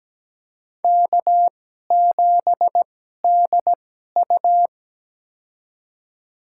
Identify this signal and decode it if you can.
Morse code, heard as a keyed tone, about 17 wpm, spaced as for 12 wpm, 700 Hz. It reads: K7DU
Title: USA Call Signs - 4 Characters - 2x Character Spacing - 17wpm